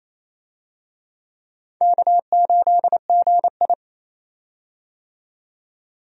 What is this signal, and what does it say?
Morse code, heard as a keyed tone, about 28 wpm, 700 Hz. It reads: K8GI
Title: USA Call Signs - 4 Characters - 28wpm